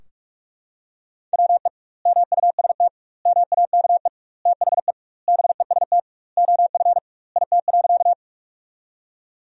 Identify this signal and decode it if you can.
Morse code, heard as a keyed tone, about 46 wpm, 700 Hz. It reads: WE MUST MAKE THE BEST OF IT.
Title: 2x Word-Spacing - Sentences from Top 400 Words - 45wpm